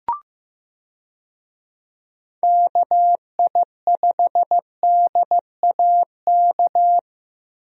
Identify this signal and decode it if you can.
Morse code, heard as a keyed tone, about 15 wpm, 700 Hz. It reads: KI5DAK
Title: USA Call Signs - 6 Characters - Repeated 3x - 15wpm